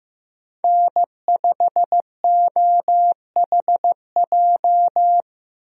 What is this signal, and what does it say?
Morse code, heard as a keyed tone, about 15 wpm, 700 Hz. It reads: N5OHJ
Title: USA Call Signs - 5 Characters - Repeated 3x - 15wpm